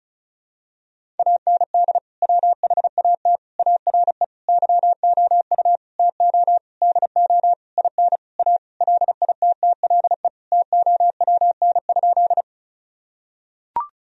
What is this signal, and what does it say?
Morse code, heard as a keyed tone, about 34 wpm, 700 Hz. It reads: AND WHAT ARE YOU TO DO IN A LITTLE TOWN?
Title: Sentences from Top 1000 Words - Review Part 2 - 35wpm